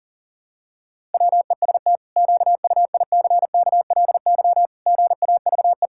Morse code, heard as a keyed tone, about 40 wpm, 700 Hz. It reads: WEST QUICKLY GAVE
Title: Pangram Challenge v2 - 40wpm